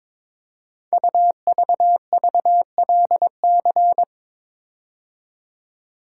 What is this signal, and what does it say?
Morse code, heard as a keyed tone, about 22 wpm, 700 Hz. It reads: UVVLC
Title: Code Groups - 5 Letters - 22wpm